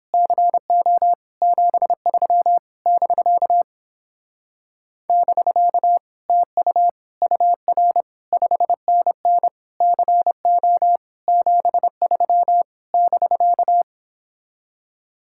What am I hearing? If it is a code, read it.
Morse code, heard as a keyed tone, about 29 wpm, 700 Hz. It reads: CO 73 <BK> <BK> TU UR 5NN CO 73 <BK>